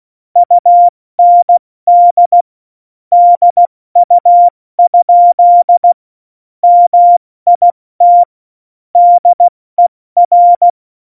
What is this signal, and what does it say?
Morse code, heard as a keyed tone, about 16 wpm, 700 Hz. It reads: UND DU? MIT DER
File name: Morsekurs / Witze / Witz 18 Juli 17, Tempo 70 F80